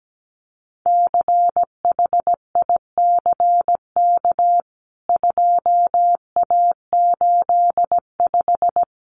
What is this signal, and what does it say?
Morse code, heard as a keyed tone, about 17 wpm, 700 Hz. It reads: CHICK 2A85
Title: CWT Exchange 17wpm